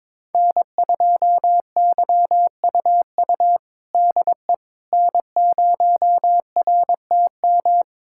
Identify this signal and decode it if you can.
Morse code, heard as a keyed tone, about 21 wpm, 700 Hz. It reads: N2YUU DE N0RTM
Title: USA Callsign DE Callsign - 5 Characters - 22wpm